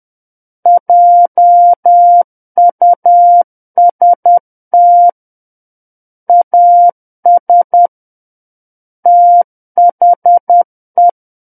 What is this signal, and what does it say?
Morse code, heard as a keyed tone, about 10 wpm, 700 Hz. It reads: JUST AS THE
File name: Book One / 01-Chapter 1 - 10 WPM